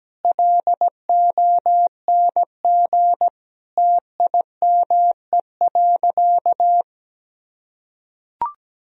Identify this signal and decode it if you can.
Morse code, heard as a keyed tone, about 17 wpm, 700 Hz. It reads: LONG TIME.
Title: Sentences from Top 1000 Words - Review Part 2 - 17wpm